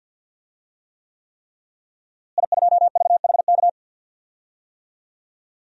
Morse code, heard as a keyed tone, about 51 wpm, 700 Hz. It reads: I2VHK
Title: DXCC-100 Call Signs - 5 Characters - Repeated 3x - 50wpm